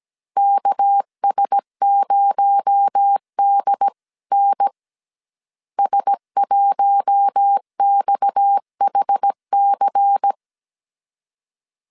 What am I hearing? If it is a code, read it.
Morse code, heard as a keyed tone, about 17 wpm, 800 Hz. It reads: KS0DN S1XHC
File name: CW / DET / 1 6 1500C S A 16wpm 2635